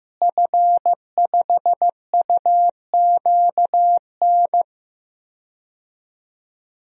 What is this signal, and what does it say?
Morse code, heard as a keyed tone, about 15 wpm, 700 Hz. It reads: F5UQN
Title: DXCC-100 Call Signs - 5 Characters - Repeated 3x - 15wpm